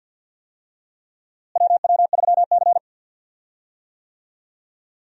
Morse code, heard as a keyed tone, about 53 wpm, 700 Hz. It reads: WW3C
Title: USA Call Signs - 4 Characters - Repeated 3x - 50wpm